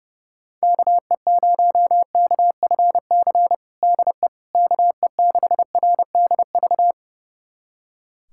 Morse code, heard as a keyed tone, about 30 wpm, 700 Hz. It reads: KE0KFC DE KE6RDV
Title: USA Callsign DE Callsign - 6 Characters - 30wpm